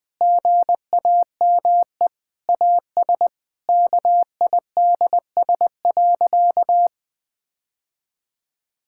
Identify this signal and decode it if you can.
Morse code, heard as a keyed tone, about 20 wpm, 700 Hz. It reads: GAME AS KIDS.
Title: Difficult Sentences with Hints - 20wpm